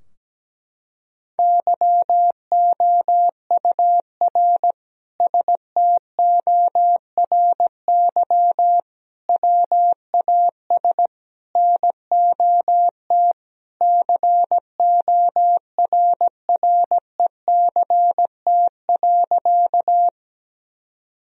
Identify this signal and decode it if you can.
Morse code, heard as a keyed tone, about 17 wpm, 700 Hz. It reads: YOUR STORY WAS NOT CORRECT.
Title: Sentences from Top 1000 Words - Review Part 2 - 17wpm